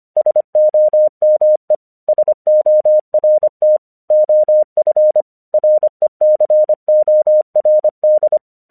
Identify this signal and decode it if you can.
Morse code, heard as a keyed tone, about 25 wpm, 610 Hz. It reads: SOME SORT OF RECORD
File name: 1 English 25wpm / AQC18